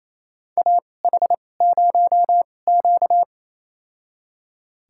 Morse code, heard as a keyed tone, about 27 wpm, spaced as for 20 wpm, 690 Hz. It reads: AH0Q
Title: USA Call Signs - 4 Characters - 2x Character Spacing - 28wpm